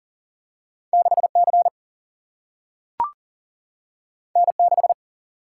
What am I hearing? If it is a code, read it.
Morse code, heard as a keyed tone, about 39 wpm, 700 Hz. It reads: 6 C N 6
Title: Single-Letter Single-Number 40wpm